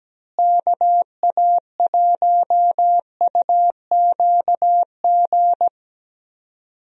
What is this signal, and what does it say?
Morse code, heard as a keyed tone, about 17 wpm, 700 Hz. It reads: KA1UQG
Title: USA Call Signs - 6 Characters - Repeated 3x - 17wpm